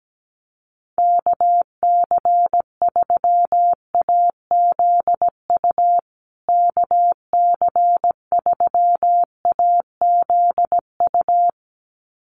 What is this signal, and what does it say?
Morse code, heard as a keyed tone, about 17 wpm, 700 Hz. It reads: KC3AZU KC3AZU
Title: USA Call Signs Repeated 2x 17wpm